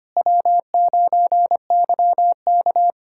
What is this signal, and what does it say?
Morse code, heard as a keyed tone, about 25 wpm, 700 Hz. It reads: W9YK
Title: USA Callsign DE Callsign - 4 Characters - 25wpm